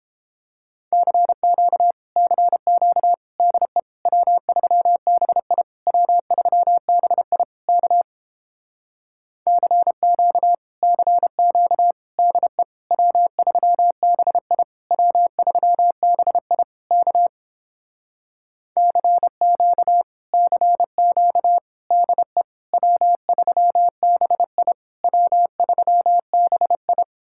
Morse code, heard as a keyed tone, about 29 wpm, 700 Hz. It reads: CQ CQ DE W3BI W3BI K CQ CQ DE W3BI W3BI K CQ CQ DE W3BI W3BI